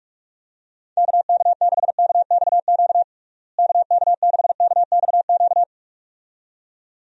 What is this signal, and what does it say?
Morse code, heard as a keyed tone, about 46 wpm, 700 Hz. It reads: KK6KXQ KK6KXQ
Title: USA Call Signs - 6 Characters - Repeated 2x - 45wpm